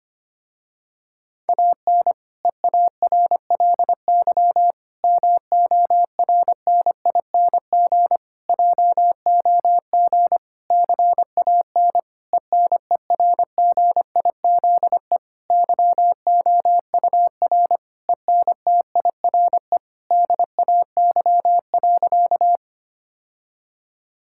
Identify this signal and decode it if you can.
Morse code, heard as a keyed tone, about 25 wpm, 700 Hz. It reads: AN EARLY MORNING JOG CAN ENERGIZE YOUR ENTIRE DAY.
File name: practice / 25wpm / Difficult Sentences with Hints - 25wpm